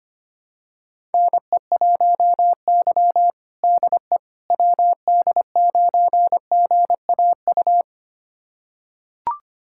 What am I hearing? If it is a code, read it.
Morse code, heard as a keyed tone, about 25 wpm, 700 Hz. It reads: NE1Y DE WD9GAU
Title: USA Callsign DE Callsign 25wpm